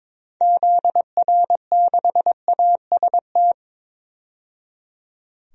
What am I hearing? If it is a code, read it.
Morse code, heard as a keyed tone, about 22 wpm, 700 Hz. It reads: ZR6AST
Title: DXCC-100 Call Signs - 6 Characters - 22wpm